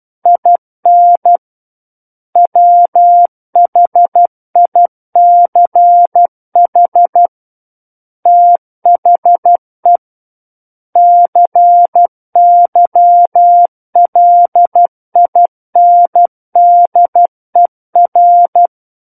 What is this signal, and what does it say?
Morse code, heard as a keyed tone, about 12 wpm, 700 Hz. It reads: IN WHICH THE CYLINDER
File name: Book One / 03-Chapter 3 - 12 WPM